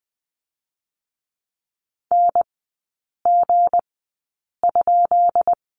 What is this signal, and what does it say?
Morse code, heard as a keyed tone, about 20 wpm, spaced as for 8 wpm, 690 Hz. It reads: NG?
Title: 159 Lesson 26e - Letter g - Three Characters 20wpm